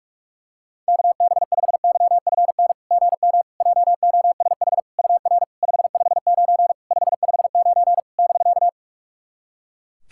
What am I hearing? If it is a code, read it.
Morse code, heard as a keyed tone, about 46 wpm, 700 Hz. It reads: KB5YFN GM JOSH UR 559 559 <BK>
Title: SOTA QSO Practice 45wpm